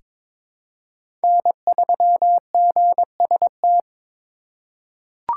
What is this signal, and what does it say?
Morse code, heard as a keyed tone, about 22 wpm, 700 Hz. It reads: N3GST
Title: USA Call Signs - 5 Characters - 4x Character Spacing - 22wpm